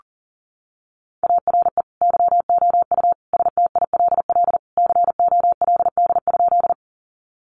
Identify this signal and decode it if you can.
Morse code, heard as a keyed tone, about 40 wpm, 700 Hz. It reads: ARE YOU STILL COLD?
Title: Sentences from Top 400 Words 40wpm